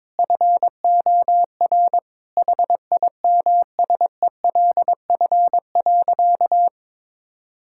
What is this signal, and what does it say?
Morse code, heard as a keyed tone, about 22 wpm, 700 Hz. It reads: FOR HIMSELF.
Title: Sentences from Top 1600 Words 22wpm